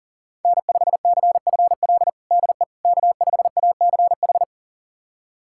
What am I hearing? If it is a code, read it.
Morse code, heard as a keyed tone, about 40 wpm, 700 Hz. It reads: N5CFL DE K5ACH
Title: USA Callsign DE Callsign - 5 Characters - 40wpm